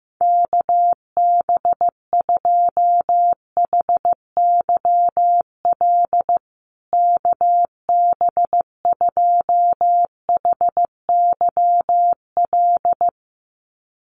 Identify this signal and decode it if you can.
Morse code, heard as a keyed tone, about 15 wpm, 700 Hz. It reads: KB2HYL KB2HYL
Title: USA Call Signs Repeated 2x 15wpm